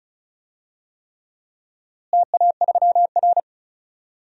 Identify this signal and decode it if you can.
Morse code, heard as a keyed tone, about 34 wpm, 700 Hz. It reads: TA3R
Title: DXCC-100 Call Signs - 4 Characters - Repeated 3x - 35wpm